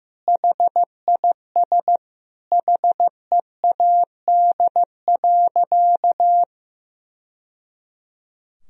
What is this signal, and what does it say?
Morse code, heard as a keyed tone, about 15 wpm, 700 Hz. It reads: HIS HEAD.
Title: Sentences from Top 1000 Words - Review Part 3 - 15wpm